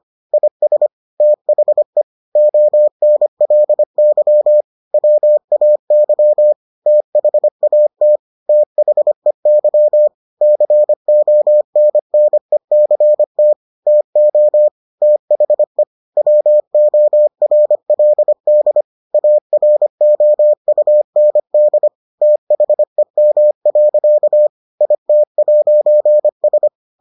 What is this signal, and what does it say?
Morse code, heard as a keyed tone, about 25 wpm, 600 Hz. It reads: IS THE ONLY WAY THAT THEY CONNECT TO THE WORLD AROUND THEM. IT'S